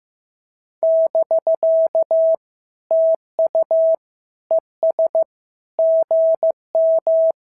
Morse code, heard as a keyed tone, about 14 wpm, 650 Hz. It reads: <BK> TU ES GM
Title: SOTA QSO Practice 15wpm